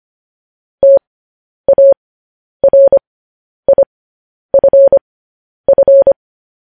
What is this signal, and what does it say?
Morse code, heard as a keyed tone, about 25 wpm, spaced as for 10 wpm, 550 Hz. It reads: TARIFF
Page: news-headlines-in-morse-code-at-10-wpm